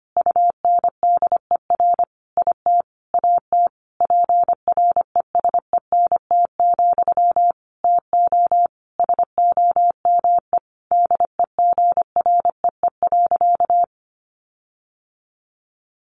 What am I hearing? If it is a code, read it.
Morse code, heard as a keyed tone, about 25 wpm, 700 Hz. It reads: UNDER IT AT PRESENT, TO SOME DEGREE.
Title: Sentences from Top 1500 Words 25wpm